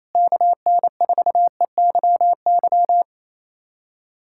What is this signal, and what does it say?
Morse code, heard as a keyed tone, about 27 wpm, 700 Hz. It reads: KN4EYY
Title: USA Callsign DE Callsign 28wpm